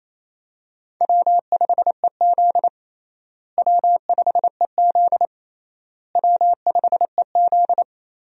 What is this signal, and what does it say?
Morse code, heard as a keyed tone, about 28 wpm, 700 Hz. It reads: W5EZ W5EZ W5EZ
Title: USA Call Signs - 4 Characters - Repeated 3x - 28wpm